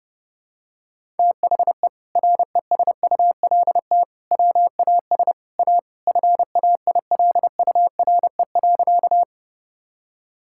Morse code, heard as a keyed tone, about 30 wpm, 700 Hz. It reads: THE RESULT WAS A FAILURE.
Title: Sentences from Top 1000 Words - Review Part 6 - 30wpm